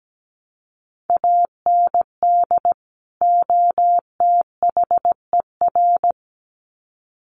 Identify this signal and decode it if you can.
Morse code, heard as a keyed tone, about 17 wpm, 700 Hz. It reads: AND OTHER
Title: Sets of 2 Words 17wpm